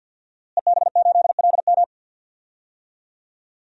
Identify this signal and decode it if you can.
Morse code, heard as a keyed tone, about 50 wpm, 700 Hz. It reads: EB8LM
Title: DXCC-100 Call Signs - 5 Characters - Repeated 3x - 50wpm